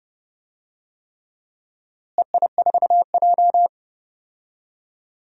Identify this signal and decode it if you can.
Morse code, heard as a keyed tone, about 30 wpm, 700 Hz. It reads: EI4J
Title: DXCC-100 Call Signs - 4 Characters - Repeated 3x - 30wpm